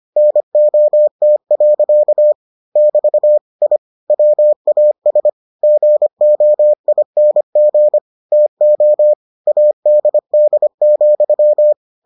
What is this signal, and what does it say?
Morse code, heard as a keyed tone, about 25 wpm, 600 Hz. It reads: NOT. <BT> I WAS GOING TO ADD,